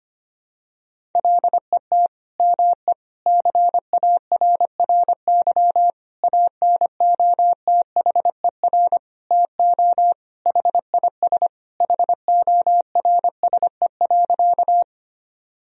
Morse code, heard as a keyed tone, about 25 wpm, 700 Hz. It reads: LET ME CARRY ANOTHER TO HIS HORSE.
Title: Sentences from Top 1000 Words - Review Part 1 - 25wpm